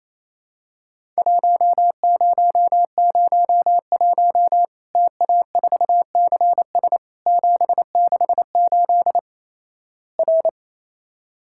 Morse code, heard as a keyed tone, about 27 wpm, 700 Hz. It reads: 1001 TA4CS 768 R
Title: Worked All Europe DX Contest - QTC Exchange - 28wpm